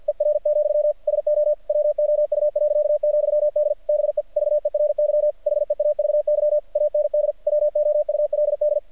Morse code, heard as a keyed tone, about 48 wpm, 600 Hz. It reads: ER, SO WOU.,D BE VERY HEAVY AND WOULD
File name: archived / CRSnet